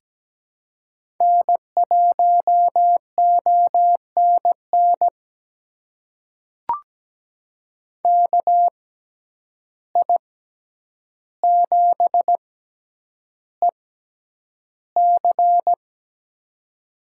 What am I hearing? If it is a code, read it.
Morse code, heard as a keyed tone, about 17 wpm, spaced as for 9 wpm, 700 Hz. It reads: N1ONN KI7EC
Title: USA Call Signs - 5 Characters - 6x Character Spacing - 17wpm